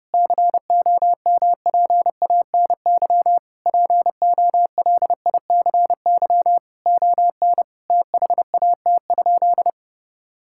Morse code, heard as a keyed tone, about 30 wpm, 700 Hz. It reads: COMPANY POLICY ON THAT?
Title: Sentences from Top 1000 Words - Review Part 5 - 30wpm